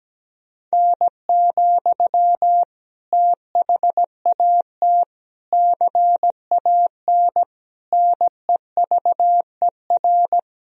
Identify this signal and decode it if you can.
Morse code, heard as a keyed tone, about 17 wpm, 700 Hz. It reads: N, THAT CAN NEVER
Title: Sentences from Top 300 Words and Abbreviations 17wpm